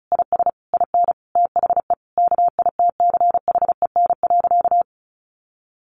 Morse code, heard as a keyed tone, about 34 wpm, 700 Hz. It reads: IS IN THE KITCHEN.
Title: Sentences from Top 1500 Words 35wpm